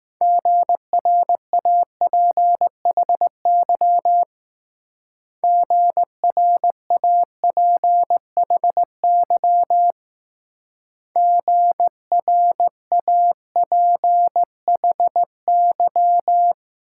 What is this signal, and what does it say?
Morse code, heard as a keyed tone, about 19 wpm, 700 Hz. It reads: GRAPHY GRAPHY GRAPHY